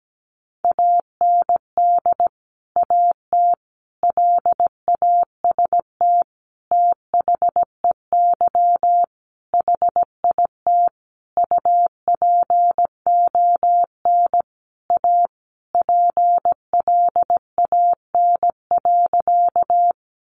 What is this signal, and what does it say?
Morse code, heard as a keyed tone, about 17 wpm, 700 Hz. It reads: AND AT LAST THEY HIT UPON A PLAN.
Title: Sentences from Top 1200 Words 17wpm